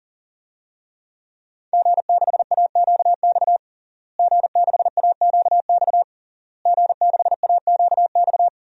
Morse code, heard as a keyed tone, about 37 wpm, 700 Hz. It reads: G6AQX G6AQX G6AQX
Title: DXCC-100 Call Signs - 5 Characters - Repeated 3x - 40wpm